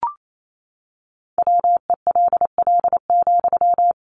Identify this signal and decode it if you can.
Morse code, heard as a keyed tone, about 28 wpm, 700 Hz. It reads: WELL,
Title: Sentences from Top 400 Words 28wpm